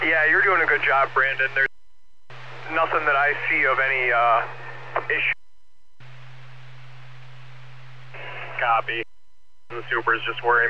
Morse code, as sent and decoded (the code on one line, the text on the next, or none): none